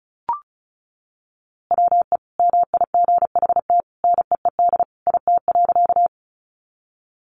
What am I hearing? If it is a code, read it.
Morse code, heard as a keyed tone, about 34 wpm, 700 Hz. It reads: WE MIGHT NEED IT.
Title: A Sound Of Thunder 35wpm